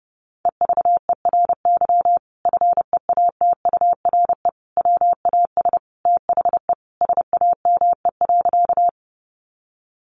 Morse code, heard as a keyed tone, about 30 wpm, 700 Hz. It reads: EVERY FEATURE WAS THE SAME.
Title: Sentences from Top 800 Words 30wpm